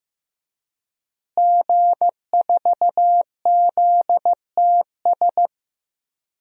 Morse code, heard as a keyed tone, about 15 wpm, 700 Hz. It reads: G4ZTS
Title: DXCC-100 Call Signs - 5 Characters - Repeated 3x - 15wpm